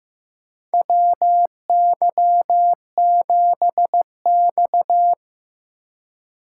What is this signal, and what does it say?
Morse code, heard as a keyed tone, about 15 wpm, 700 Hz. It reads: WY7X
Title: USA Call Signs - 4 Characters - 3x Character Spacing - 15wpm